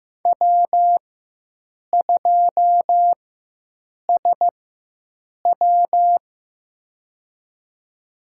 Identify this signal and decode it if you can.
Morse code, heard as a keyed tone, about 15 wpm, spaced as for 7 wpm, 700 Hz. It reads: W2SW
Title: USA Call Signs - 4 Characters - 4x Character Spacing - 15wpm